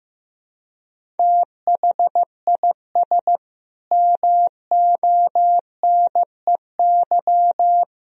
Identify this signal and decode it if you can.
Morse code, heard as a keyed tone, about 15 wpm, 700 Hz. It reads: THIS MONEY
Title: Sentences from Top 400 Words and Abbreviations 15wpm